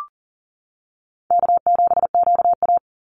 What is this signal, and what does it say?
Morse code, heard as a keyed tone, about 40 wpm, 700 Hz. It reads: K7QA
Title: Call Signs - 40wpm